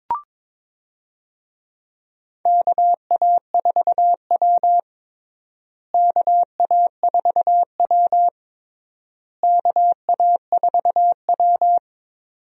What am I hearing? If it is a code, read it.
Morse code, heard as a keyed tone, about 22 wpm, 700 Hz. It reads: KA4W KA4W KA4W
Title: USA Call Signs - 4 Characters - Repeated 3x - 22wpm